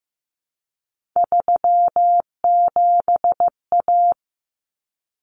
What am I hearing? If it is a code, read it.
Morse code, heard as a keyed tone, about 15 wpm, 700 Hz. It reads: 37A
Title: Common Cut Numbers - 3-Digit 15wpm